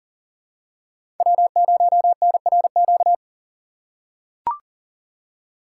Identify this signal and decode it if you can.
Morse code, heard as a keyed tone, about 38 wpm, 700 Hz. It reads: W0NRQ
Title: USA Call Signs - 6x Character Spacing - 40wpm